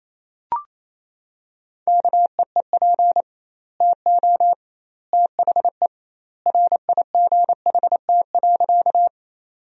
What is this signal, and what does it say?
Morse code, heard as a keyed tone, about 28 wpm, 700 Hz. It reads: KEEP TO THE RIGHT.
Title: 2x Word-Spacing - Sentences from Top 300 Words - 28wpm